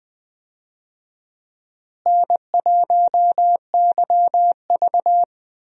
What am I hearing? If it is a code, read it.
Morse code, heard as a keyed tone, about 20 wpm, 700 Hz. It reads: N1YV
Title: USA Call Signs - 4 Characters - Repeated 2x - 20wpm